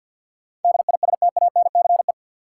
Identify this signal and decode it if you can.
Morse code, heard as a keyed tone, about 53 wpm, 700 Hz. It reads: DISTANCE
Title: Sentences from Top 1200 Words - New Word Drill - 50wpm